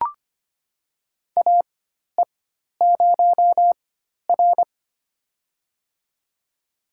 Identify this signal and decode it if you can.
Morse code, heard as a keyed tone, about 25 wpm, spaced as for 12 wpm, 700 Hz. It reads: AE0R
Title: USA Call Signs - 4 Characters - 4x Character Spacing - 25wpm